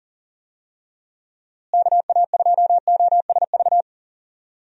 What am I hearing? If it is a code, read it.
Morse code, heard as a keyed tone, about 40 wpm, 700 Hz. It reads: KA2OSV
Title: USA Call Signs - 6 Characters - Repeated 2x - 40wpm